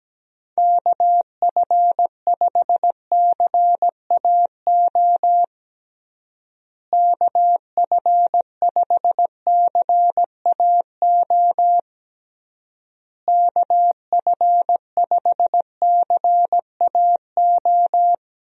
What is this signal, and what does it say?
Morse code, heard as a keyed tone, about 17 wpm, 700 Hz. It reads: KF5CAO KF5CAO KF5CAO
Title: USA Call Signs - 6 Characters - Repeated 3x - 17wpm